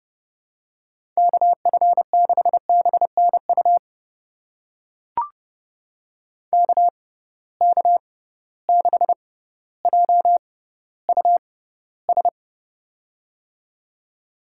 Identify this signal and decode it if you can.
Morse code, heard as a keyed tone, about 30 wpm, spaced as for 15 wpm, 700 Hz. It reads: KF6BNU KK6JUS
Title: USA Call Signs - 6 Characters - 6x Character Spacing - 30wpm